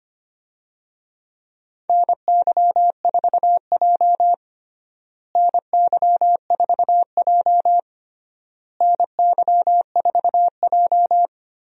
Morse code, heard as a keyed tone, about 25 wpm, 700 Hz. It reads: NY4J NY4J NY4J
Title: USA Call Signs - 4 Characters - Repeated 3x - 25wpm